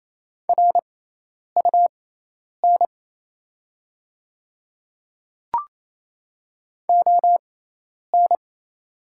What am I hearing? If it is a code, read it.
Morse code, heard as a keyed tone, about 28 wpm, spaced as for 10 wpm, 700 Hz. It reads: RUN ON
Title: 6x Character Spacing - Top 300 Words - 28wpm